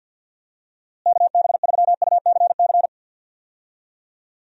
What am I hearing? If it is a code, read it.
Morse code, heard as a keyed tone, about 51 wpm, 700 Hz. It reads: KB3UCC
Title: USA Call Signs - 6 Characters - Repeated 3x - 50wpm